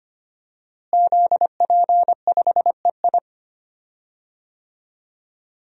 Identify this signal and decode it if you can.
Morse code, heard as a keyed tone, about 25 wpm, 700 Hz. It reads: ZP5EI
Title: DXCC-100 Call Signs - 5 Characters - Repeated 2x - 25wpm